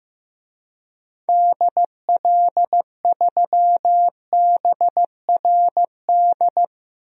Morse code, heard as a keyed tone, about 16 wpm, 700 Hz. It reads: DL3BRD
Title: DXCC-100 Call Signs - 6 Characters - 15wpm